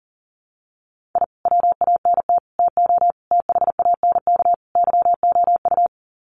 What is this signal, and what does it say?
Morse code, heard as a keyed tone, about 37 wpm, 700 Hz. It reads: I WANT TO THANK YOU
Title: Sets of 5 Words Encore 40wpm